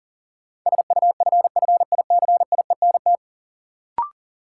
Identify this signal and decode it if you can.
Morse code, heard as a keyed tone, about 40 wpm, 700 Hz. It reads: SUFFICIENT
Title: Sentences from Top 1400 Words - New Word Drill Alt - 40wpm